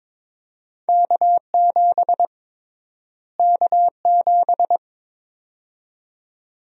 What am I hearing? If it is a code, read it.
Morse code, heard as a keyed tone, about 22 wpm, 700 Hz. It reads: K7 K7
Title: USA Call Sign Prefixes - Repeated 2x - 22wpm